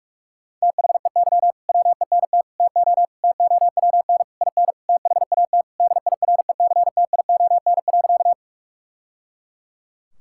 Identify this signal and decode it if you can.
Morse code, heard as a keyed tone, about 42 wpm, 700 Hz. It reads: THEY WENT TO TOWN IN THAT DIRECTION.
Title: Sentences from Top 1000 Words - Review Part 5 - 45wpm